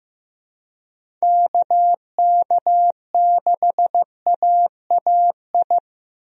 Morse code, heard as a keyed tone, about 16 wpm, 700 Hz. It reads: KK6AAI
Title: USA Call Signs - 6 Characters - Repeated 3x - 15wpm